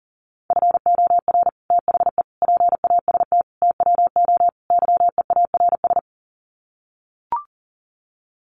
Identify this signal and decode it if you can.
Morse code, heard as a keyed tone, about 40 wpm, 700 Hz. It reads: FOR THE PAST TWO YEARS
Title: Sets of 5 Words Encore2 40wpm